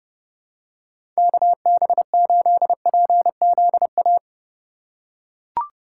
Morse code, heard as a keyed tone, about 30 wpm, 700 Hz. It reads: KB8PZA
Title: USA Call Signs - 6 Characters - 3x Character Spacing - 30wpm